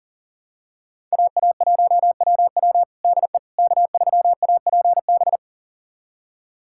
Morse code, heard as a keyed tone, about 40 wpm, 700 Hz. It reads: AA1WW DE K3APB
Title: USA Callsign DE Callsign - 5 Characters - 40wpm